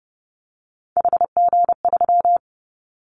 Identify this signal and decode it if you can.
Morse code, heard as a keyed tone, about 30 wpm, 700 Hz. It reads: HG3
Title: Callsign Prefixes for Top 25 DXCC - 30wpm